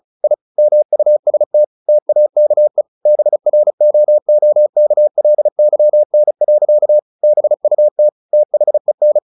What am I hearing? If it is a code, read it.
Morse code, heard as a keyed tone, about 34 wpm, 600 Hz. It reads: I MUST TAKE BROOKLYN. BUT THEN
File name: the old man and the sea / the old man and the sea 1 35wpm